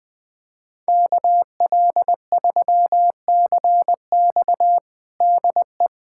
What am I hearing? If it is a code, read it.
Morse code, heard as a keyed tone, about 21 wpm, 700 Hz. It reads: KL3CX DE
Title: USA Callsign DE Callsign 20wpm